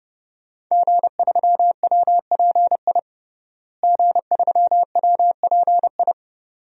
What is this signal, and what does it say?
Morse code, heard as a keyed tone, about 30 wpm, 700 Hz. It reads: G3WPI G3WPI
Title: DXCC-100 Call Signs - 5 Characters - Repeated 2x - 30wpm